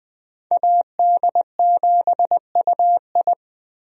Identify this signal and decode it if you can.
Morse code, heard as a keyed tone, about 20 wpm, 700 Hz. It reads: AD7UI
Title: USA Call Signs - 5 Characters - Repeated 3x - 20wpm